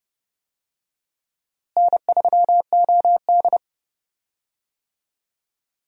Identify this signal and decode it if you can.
Morse code, heard as a keyed tone, about 32 wpm, 700 Hz. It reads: N3OD
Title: USA Call Signs - 4 Characters - Repeated 2x - 30wpm